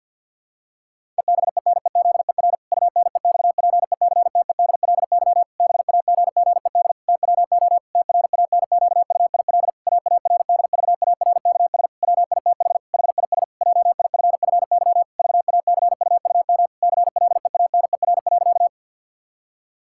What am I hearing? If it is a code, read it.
Morse code, heard as a keyed tone, about 52 wpm, 700 Hz. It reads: EBENEZER UNEXPECTEDLY BAGGED TWO TRANQUIL AARDVARKS WITH HIS JIFFY VACUUM CLEANER.
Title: Pangram Challenge v2 - 50wpm